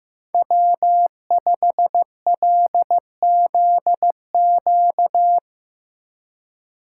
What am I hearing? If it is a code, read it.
Morse code, heard as a keyed tone, about 16 wpm, 700 Hz. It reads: W5LZQ
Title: Speed Racing - USA 1x3 Call Signs 15wpm